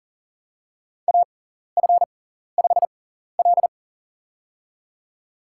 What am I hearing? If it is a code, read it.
Morse code, heard as a keyed tone, about 40 wpm, spaced as for 14 wpm, 700 Hz. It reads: AF5L
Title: USA Call Signs - 4 Characters - 6x Character Spacing - 40wpm